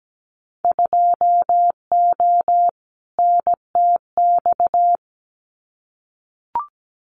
Morse code, heard as a keyed tone, about 17 wpm, 700 Hz. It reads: 2O NTX
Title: Winter Field Day Exchanges 17wpm